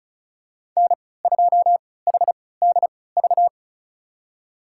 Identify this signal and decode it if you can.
Morse code, heard as a keyed tone, about 36 wpm, spaced as for 20 wpm, 690 Hz. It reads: N2HDV
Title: USA Call Signs - 5 Characters - 3x Character Spacing - 35wpm